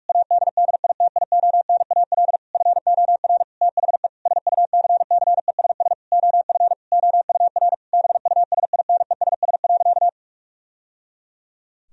Morse code, heard as a keyed tone, about 46 wpm, 700 Hz. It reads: ADDITIONAL FOR THE SUCCESS OF OUR BUSINESS.
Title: Sentences from Top 1000 Words - Review Part 4 - 45wpm